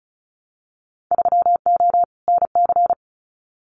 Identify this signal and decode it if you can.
Morse code, heard as a keyed tone, about 36 wpm, 700 Hz. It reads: 3O NC
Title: Winter Field Day Exchanges 35wpm